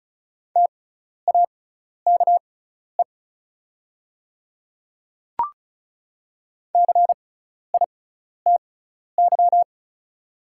Morse code, heard as a keyed tone, about 34 wpm, spaced as for 12 wpm, 690 Hz. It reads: TAKE CITY
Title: 6x Character Spacing - Top 300 Words - 35wpm